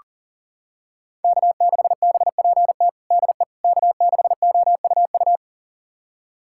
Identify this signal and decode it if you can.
Morse code, heard as a keyed tone, about 40 wpm, 700 Hz. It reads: K6BPT DE K6OUU
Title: USA Callsign DE Callsign - 5 Characters - 40wpm